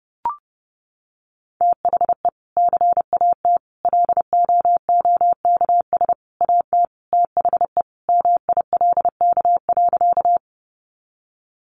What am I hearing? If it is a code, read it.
Morse code, heard as a keyed tone, about 30 wpm, 700 Hz. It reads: THE CAT LOOKS AT THE MILK.
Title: Additional Easy Phrases 30wpm